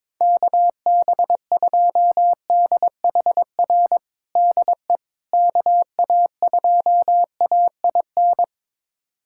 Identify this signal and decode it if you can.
Morse code, heard as a keyed tone, about 22 wpm, 700 Hz. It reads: KB2DHR DE KA2AIN
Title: USA Callsign DE Callsign - 6 Characters - 22wpm